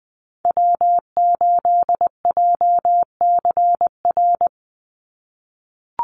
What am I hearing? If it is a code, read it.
Morse code, heard as a keyed tone, about 20 wpm, 700 Hz. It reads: W8JCR
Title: USA Call Signs 20wpm